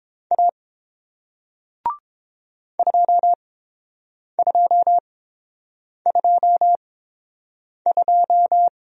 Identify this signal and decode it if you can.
Morse code, heard as a keyed tone, about 31 wpm, spaced as for 7 wpm, 700 Hz. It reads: A 2222